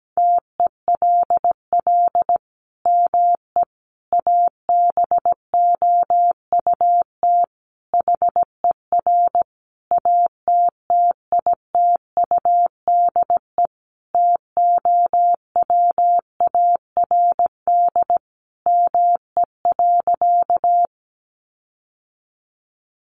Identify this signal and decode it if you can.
Morse code, heard as a keyed tone, about 17 wpm, 700 Hz. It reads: TELL ME ABOUT HER ATTITUDE TOWARD ME.
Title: Sentences from Top 1300 Words 17wpm